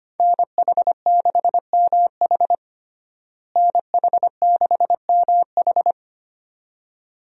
Morse code, heard as a keyed tone, about 25 wpm, 700 Hz. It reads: NH6MH NH6MH
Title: USA Call Signs - 5 Characters - Repeated 2x - 25wpm